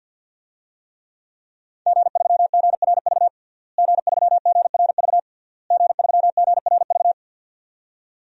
Morse code, heard as a keyed tone, about 52 wpm, 700 Hz. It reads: G3GRV G3GRV G3GRV
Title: DXCC-100 Call Signs - 5 Characters - Repeated 3x - 50wpm